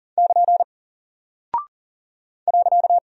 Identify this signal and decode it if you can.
Morse code, heard as a keyed tone, about 40 wpm, 700 Hz. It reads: <KN> .
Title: Single Character (Letter:Number:Punctuation:Prosign) 40wpm